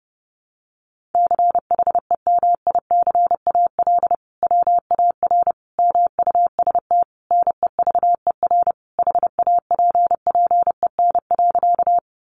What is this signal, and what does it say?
Morse code, heard as a keyed tone, about 30 wpm, 700 Hz. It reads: CHEMICAL WAR MUST NEVER HAPPEN.
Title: Sentences from Top 1200 Words 30wpm